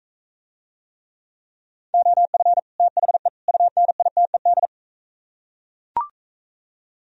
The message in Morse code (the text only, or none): OF THE UNITED